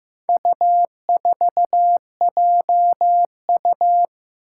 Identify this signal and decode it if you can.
Morse code, heard as a keyed tone, about 15 wpm, 700 Hz. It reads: U4JU
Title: Four Challenging Letter-Number Combinations 15wpm